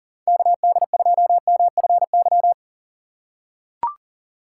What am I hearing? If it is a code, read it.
Morse code, heard as a keyed tone, about 40 wpm, 700 Hz. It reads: KD2MFY
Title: USA Call Signs - 6 Characters - 3x Character Spacing - 40wpm